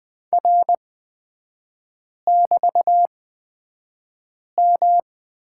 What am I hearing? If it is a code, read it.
Morse code, heard as a keyed tone, about 20 wpm, spaced as for 5 wpm, 700 Hz. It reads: R<BT>M
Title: 224 Lesson 36e - BT - Three Characters 20wpm